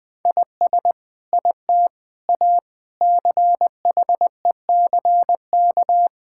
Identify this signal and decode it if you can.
Morse code, heard as a keyed tone, about 20 wpm, 700 Hz. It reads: IS IT A CHECK
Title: Sentences from Top 1000 Words - Review Part 6 - 20wpm